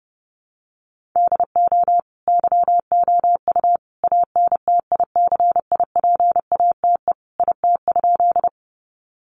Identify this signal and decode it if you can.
Morse code, heard as a keyed tone, about 30 wpm, 700 Hz. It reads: DO YOU ANTICIPATE IT?
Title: Sentences from Top 1500 Words 30wpm